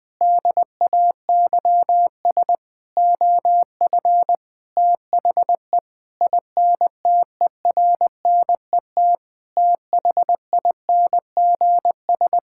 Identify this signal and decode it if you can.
Morse code, heard as a keyed tone, about 20 wpm, 700 Hz. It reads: DAYS OF THE INTERNET THINGS
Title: Sentences with 6-Word Phrases - Spoken Beginning - 20wpm